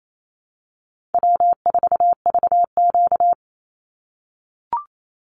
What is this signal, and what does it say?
Morse code, heard as a keyed tone, about 28 wpm, 700 Hz. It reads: W4VQ
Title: Easy Call Signs - 28wpm